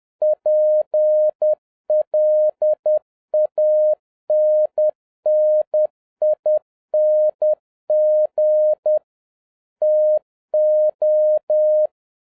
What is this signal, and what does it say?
Morse code, heard as a keyed tone, about 10 wpm, 610 Hz. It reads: PLANNING TO